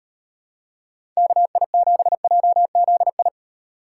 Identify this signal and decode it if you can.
Morse code, heard as a keyed tone, about 38 wpm, 700 Hz. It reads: KI7JZI